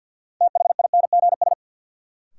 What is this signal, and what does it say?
Morse code, heard as a keyed tone, about 48 wpm, 700 Hz. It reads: THINGS
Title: Sentences from Top 1400 Words - New Word Drill Alt - 50wpm